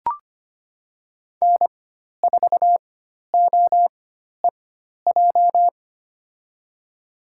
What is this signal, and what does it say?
Morse code, heard as a keyed tone, about 25 wpm, spaced as for 12 wpm, 700 Hz. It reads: N4OEJ
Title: USA Call Signs - 5 Characters - 4x Character Spacing - 25wpm